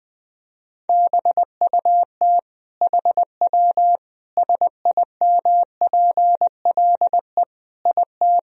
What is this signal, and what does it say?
Morse code, heard as a keyed tone, about 20 wpm, 700 Hz. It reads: BUT HW SIMPLE IT
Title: Sentences from Top 400 Words and Abbreviations 20wpm